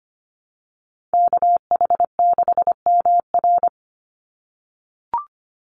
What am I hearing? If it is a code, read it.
Morse code, heard as a keyed tone, about 25 wpm, 700 Hz. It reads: KH6MR
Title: USA Call Signs 25wpm